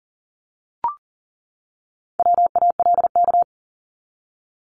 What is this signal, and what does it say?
Morse code, heard as a keyed tone, about 40 wpm, 700 Hz. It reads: WALK
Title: Top 300 Words 40wpm